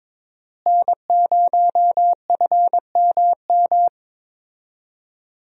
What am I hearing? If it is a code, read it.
Morse code, heard as a keyed tone, about 22 wpm, 700 Hz. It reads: N0FMM
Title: USA Call Signs - 5 Characters - Repeated 3x - 22wpm